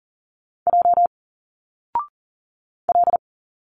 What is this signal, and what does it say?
Morse code, heard as a keyed tone, about 40 wpm, 700 Hz. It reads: J L
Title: Single Letters 40wpm